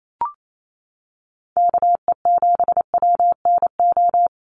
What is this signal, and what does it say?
Morse code, heard as a keyed tone, about 28 wpm, 700 Hz. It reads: KE7WNO
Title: USA Call Signs 28wpm